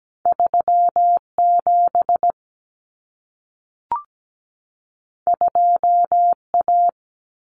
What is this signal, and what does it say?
Morse code, heard as a keyed tone, about 17 wpm, 700 Hz. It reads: 37 2A
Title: Common Cut Numbers - 2-Digit 17wpm